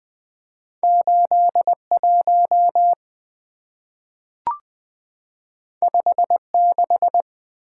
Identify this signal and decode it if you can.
Morse code, heard as a keyed tone, about 20 wpm, 700 Hz. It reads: 81 56
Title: Two Letter-Number Combinations - 20wpm